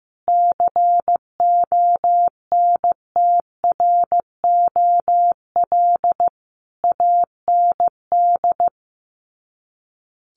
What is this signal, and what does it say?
Morse code, heard as a keyed tone, about 15 wpm, 700 Hz. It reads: CONTROL AND
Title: Sets of 5 Words Encore2 15wpm